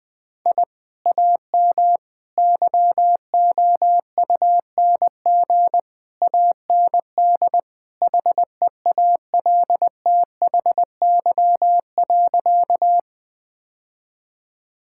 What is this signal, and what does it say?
Morse code, heard as a keyed tone, about 20 wpm, 700 Hz. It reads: I AM YOUNG AND HEALTHY.
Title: Sentences from Top 1000 Words - Review Part 5 - 20wpm